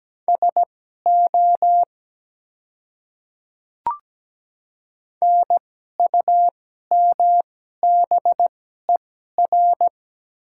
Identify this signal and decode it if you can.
Morse code, heard as a keyed tone, about 17 wpm, spaced as for 12 wpm, 700 Hz. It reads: SO NUMBER
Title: 2x Character Spacing - Top 100 Words - 17wpm